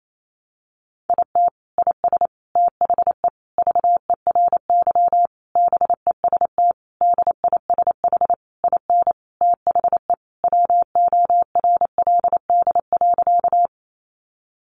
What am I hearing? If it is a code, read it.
Morse code, heard as a keyed tone, about 28 wpm, 700 Hz. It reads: IT IS THE VERY BEST DISH IN THE WORLD.
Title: Sentences from Top 1300 Words 28wpm